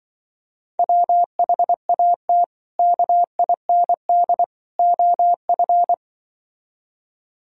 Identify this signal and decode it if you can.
Morse code, heard as a keyed tone, about 24 wpm, 700 Hz. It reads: WHAT KIND OF
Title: Speed-Racing - Sets of 3 Words 20wpm